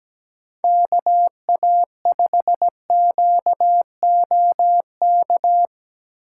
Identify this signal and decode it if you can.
Morse code, heard as a keyed tone, about 19 wpm, 700 Hz. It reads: KA5QOK
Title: Speed Racing - USA 2x3 Call Signs 17wpm